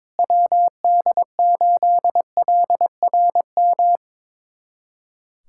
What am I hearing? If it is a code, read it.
Morse code, heard as a keyed tone, about 22 wpm, 700 Hz. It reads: WD8LRM
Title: USA Callsign DE Callsign - 6 Characters - 22wpm